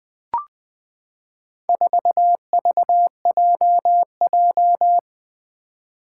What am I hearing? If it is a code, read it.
Morse code, heard as a keyed tone, about 20 wpm, 700 Hz. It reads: 4VJJ
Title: Four Challenging Letter-Number Combinations 20wpm